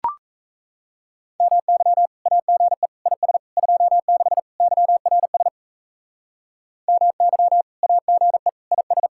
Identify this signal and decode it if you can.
Morse code, heard as a keyed tone, about 34 wpm, 700 Hz. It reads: MY AGE IS 26 YRS MY AGE IS
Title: Speed-Racing Common QSO Phrases - 28wpm